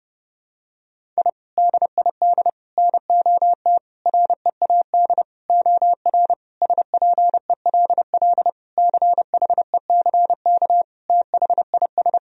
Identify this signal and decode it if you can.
Morse code, heard as a keyed tone, about 29 wpm, 700 Hz. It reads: I DID NOT READ OR SPELL CHECK THIS
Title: Sentences from Top 1000 Words - Review Part 3 - 30wpm